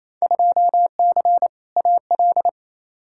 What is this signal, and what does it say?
Morse code, heard as a keyed tone, about 28 wpm, 700 Hz. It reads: 2C AL
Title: Field Day Exchanges Encore - 28wpm